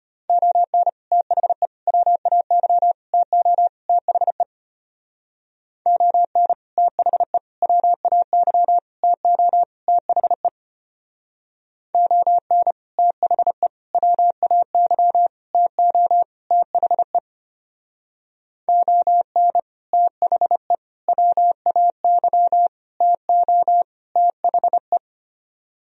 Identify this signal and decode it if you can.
Morse code, heard as a keyed tone, about 31 wpm, 700 Hz. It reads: ON THE WAY TO THE ON THE WAY TO THE ON THE WAY TO THE ON THE WAY TO THE
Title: Speed-Racing - Sets of 5 Words 25wpm